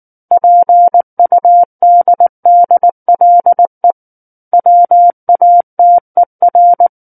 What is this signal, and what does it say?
Morse code, heard as a keyed tone, about 19 wpm, 700 Hz. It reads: PUDDLE WATER
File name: Book One / 10-Chapter 10 - 19 WPM